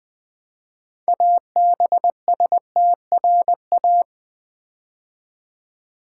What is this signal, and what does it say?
Morse code, heard as a keyed tone, about 20 wpm, 700 Hz. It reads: ABSTRA
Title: Build A Word - 3 Letters At A Time - 8-Letter Words - 20wpm